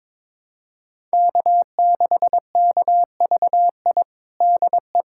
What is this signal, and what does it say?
Morse code, heard as a keyed tone, about 22 wpm, 700 Hz. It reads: K6KVI DE
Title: USA Callsign DE Callsign - 5 Characters - 22wpm